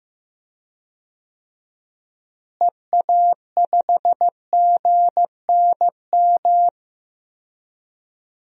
Morse code, heard as a keyed tone, about 15 wpm, 700 Hz. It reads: EA5GNM
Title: DXCC-100 Call Signs - 6 Characters - Repeated 2x - 15wpm